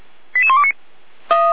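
Five-tone selective call (ZVEI), heard as quick runs of five tones, tones of about 70 ms. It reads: 88129
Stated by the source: ZVEI five-tone call received on SDR.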